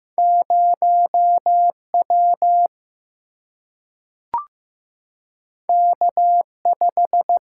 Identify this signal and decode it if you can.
Morse code, heard as a keyed tone, about 15 wpm, 700 Hz. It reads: 0 W K 5
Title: Single-Letter Single-Number 15wpm